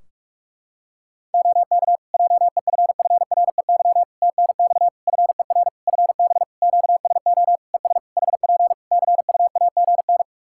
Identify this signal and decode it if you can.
Morse code, heard as a keyed tone, about 48 wpm, 700 Hz. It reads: OK JEFFREY TNX FER FB QSO ES HP CUAGN
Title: Practice QSO - Exchange 4 - Ending - 45wpm